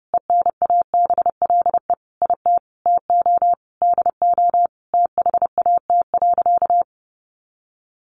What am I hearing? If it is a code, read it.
Morse code, heard as a keyed tone, about 30 wpm, 700 Hz. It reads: ENABLE IT TO DO THAT.
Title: Sentences from Top 1500 Words 30wpm